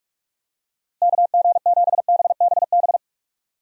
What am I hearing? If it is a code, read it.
Morse code, heard as a keyed tone, about 46 wpm, 700 Hz. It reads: KG7BBB
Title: USA Call Signs - 6 Characters - Repeated 2x - 45wpm